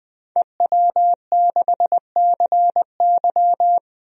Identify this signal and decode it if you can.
Morse code, heard as a keyed tone, about 20 wpm, 700 Hz. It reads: EW6CY
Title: DXCC-100 Call Signs - 5 Characters - 20wpm